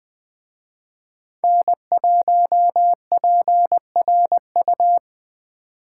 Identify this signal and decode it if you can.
Morse code, heard as a keyed tone, about 20 wpm, 700 Hz. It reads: N1PRU